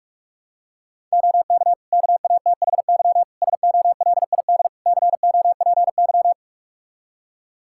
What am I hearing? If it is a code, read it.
Morse code, heard as a keyed tone, about 45 wpm, 700 Hz. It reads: OK KATHY SOLID COPY
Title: Practice QSO - Exchange 3 - More Info - 45wpm